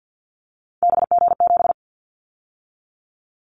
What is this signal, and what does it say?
Morse code, heard as a keyed tone, about 50 wpm, 700 Hz. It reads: BG7
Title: Callsign Prefixes for Top 100 DXCC - 50wpm